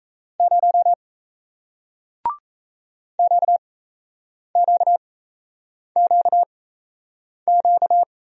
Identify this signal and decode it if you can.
Morse code, heard as a keyed tone, about 37 wpm, spaced as for 8 wpm, 700 Hz. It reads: 0 QQQQ